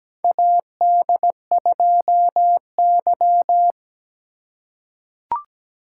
Morse code, heard as a keyed tone, about 17 wpm, 700 Hz. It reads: AD2Y
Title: USA Call Signs - 4 Characters - 2x Character Spacing - 17wpm